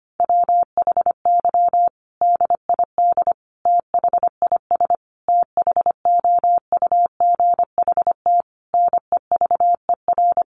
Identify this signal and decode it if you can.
Morse code, heard as a keyed tone, about 25 wpm, 700 Hz. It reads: WHY DID THIS THOUGHT NEVER
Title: Sentences from Top 900 Words 25wpm